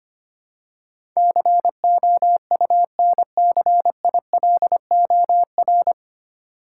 Morse code, heard as a keyed tone, about 25 wpm, 700 Hz. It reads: COUNCILOR
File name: Nine Letter Words - 25wpm